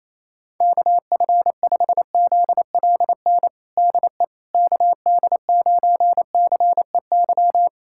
Morse code, heard as a keyed tone, about 27 wpm, 700 Hz. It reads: KF5ZLN DE KD9CEY
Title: USA Callsign DE Callsign 28wpm